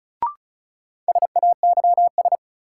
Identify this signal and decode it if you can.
Morse code, heard as a keyed tone, about 36 wpm, 700 Hz. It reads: SAYS
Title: Sets of 2 Words - New Word Drill - 35wpm